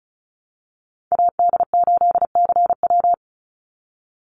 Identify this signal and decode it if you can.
Morse code, heard as a keyed tone, about 34 wpm, 700 Hz. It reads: AD8CW
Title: Call Signs - 35wpm